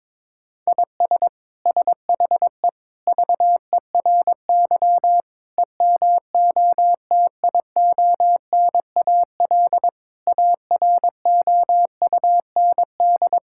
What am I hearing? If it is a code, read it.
Morse code, heard as a keyed tone, about 22 wpm, 700 Hz. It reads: IS SHE VERY EMOTIONAL AROUND
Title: Sentences from Top 1000 Words - Review Part 5 - 22wpm